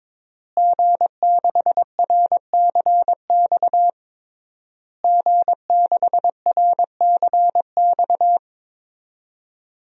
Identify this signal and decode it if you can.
Morse code, heard as a keyed tone, about 22 wpm, 700 Hz. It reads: G6RCX G6RCX
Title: DXCC-100 Call Signs - 5 Characters - Repeated 3x - 22wpm